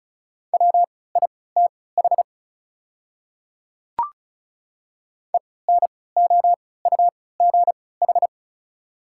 Morse code, heard as a keyed tone, about 34 wpm, spaced as for 20 wpm, 700 Hz. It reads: WITH ENOUGH
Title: 3x Character Spacing - Top 500 Words - 35wpm